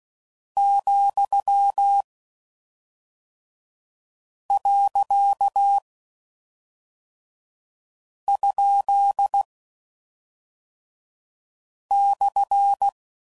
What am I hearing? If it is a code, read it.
Morse code, heard as a keyed tone, about 16 wpm, 800 Hz. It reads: , . ? /
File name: Mixed / Misc / Words